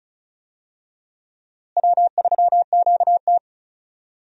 Morse code, heard as a keyed tone, about 34 wpm, 700 Hz. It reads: W3QT
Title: USA Call Signs - 4 Characters - Repeated 2x - 35wpm